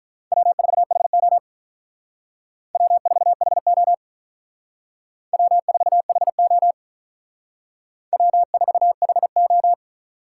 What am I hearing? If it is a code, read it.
Morse code, heard as a keyed tone, about 44 wpm, 700 Hz. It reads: W4HO W4HO W4HO W4HO
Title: Speed Racing - USA 1x2 Call Signs 35wpm